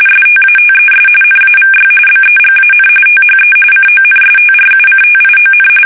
RTTY, about 45.45 baud. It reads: STATES NAVY BEGAN THE CONSTRUCTION
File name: fsk850 / battleships